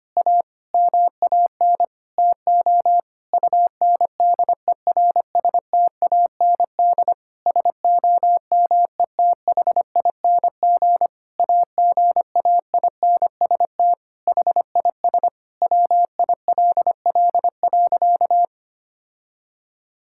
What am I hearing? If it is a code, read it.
Morse code, heard as a keyed tone, about 25 wpm, 700 Hz. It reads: A MAN TO UNDERSTAND SOMETHING AGAINST HIS WILL.
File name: practice / 25wpm / Sentences with 9-Word Phrases - 25wpm